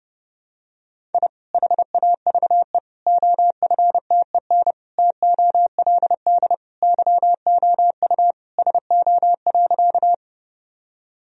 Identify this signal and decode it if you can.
Morse code, heard as a keyed tone, about 32 wpm, 700 Hz. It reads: I HAVE OFTEN TOLD YOU SO.
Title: Sentences from Top 1000 Words - Review Part 1 - 30wpm